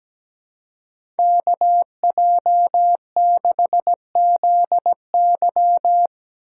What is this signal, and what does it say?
Morse code, heard as a keyed tone, about 17 wpm, 700 Hz. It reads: KJ6ZY
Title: USA Call Signs - 5 Characters - 2x Character Spacing - 17wpm